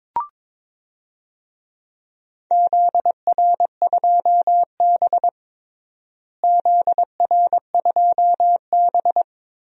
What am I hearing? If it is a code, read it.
Morse code, heard as a keyed tone, about 22 wpm, 700 Hz. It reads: ZR2B ZR2B
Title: DXCC-100 Call Signs - 4 Characters - Repeated 3x - 22wpm